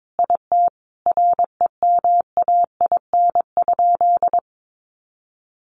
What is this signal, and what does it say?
Morse code, heard as a keyed tone, about 22 wpm, 700 Hz. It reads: IT REMAIN?
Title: Sentences from Top 1200 Words 22wpm